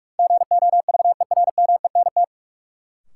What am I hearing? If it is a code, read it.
Morse code, heard as a keyed tone, about 46 wpm, 700 Hz. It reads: GOVERMENT
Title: Sentences from Top 1300 Words - New Word Drill - 45wpm